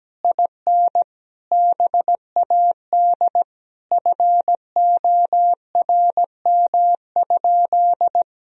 Morse code, heard as a keyed tone, about 17 wpm, 700 Hz. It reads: IN BAD FORM?
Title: Sentences from Top 1000 Words - Review Part 1 - 17wpm